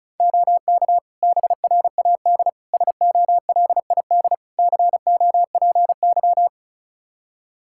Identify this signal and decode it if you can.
Morse code, heard as a keyed tone, about 34 wpm, 700 Hz. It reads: OK BRAD SOLID COPY
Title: K1USN SST QSO Practice 35wpm